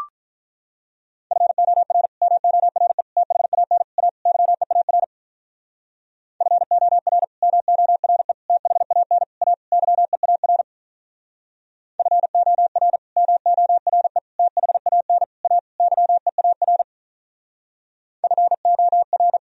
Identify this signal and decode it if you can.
Morse code, heard as a keyed tone, about 43 wpm, 700 Hz. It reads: FOR MORE THAN A YEAR FOR MORE THAN A YEAR FOR MORE THAN A YEAR FOR
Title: Speed-Racing - Sets of 5 Words 35wpm